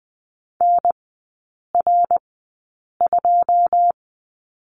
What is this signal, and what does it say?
Morse code, heard as a keyed tone, about 20 wpm, spaced as for 8 wpm, 700 Hz. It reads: NR2
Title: 133 Lesson 22e - Question Mark - Three Characters 20wpm